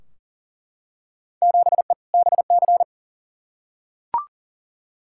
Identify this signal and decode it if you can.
Morse code, heard as a keyed tone, about 40 wpm, 700 Hz. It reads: 7E BC
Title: Field Day Exchanges Encore - 40wpm